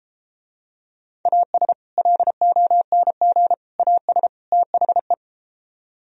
Morse code, heard as a keyed tone, about 33 wpm, 700 Hz. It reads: AS LONG AS THE